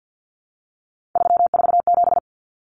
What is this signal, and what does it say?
Morse code, heard as a keyed tone, about 49 wpm, 700 Hz. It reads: 347
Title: Numbers - 3-Digit 50wpm